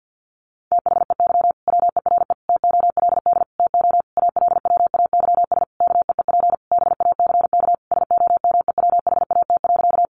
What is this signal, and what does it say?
Morse code, heard as a keyed tone, about 50 wpm, 700 Hz. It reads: THEY WERE TOLD TO ALWAYS KEEP BACK SOMEWHAT.
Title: Sentences from Top 1300 Words 50wpm